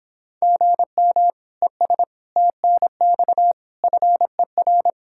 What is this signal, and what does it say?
Morse code, heard as a keyed tone, about 26 wpm, 700 Hz. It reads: GM ES TNX FER